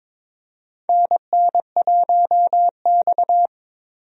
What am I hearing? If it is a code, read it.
Morse code, heard as a keyed tone, about 22 wpm, 700 Hz. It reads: NN1X
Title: USA Call Signs - 4 Characters - Repeated 2x - 22wpm